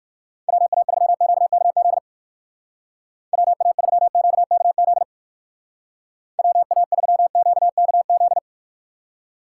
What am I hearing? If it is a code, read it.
Morse code, heard as a keyed tone, about 51 wpm, 700 Hz. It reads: WA3QKZ WA3QKZ WA3QKZ
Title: Speed Racing - USA 2x3 Call Signs 45wpm